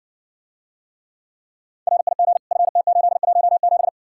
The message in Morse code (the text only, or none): RIG FT817